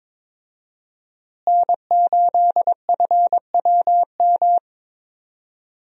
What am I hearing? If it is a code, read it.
Morse code, heard as a keyed tone, about 21 wpm, 700 Hz. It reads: N8FWM
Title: USA Call Signs - 5 Characters - Repeated 2x - 22wpm